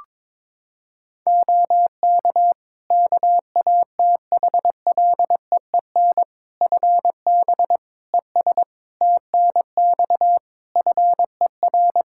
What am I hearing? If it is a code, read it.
Morse code, heard as a keyed tone, about 22 wpm, 700 Hz. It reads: OK KATHLEEN FB ES TNX FER
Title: Practice QSO - Exchange 2 - Info - 22wpm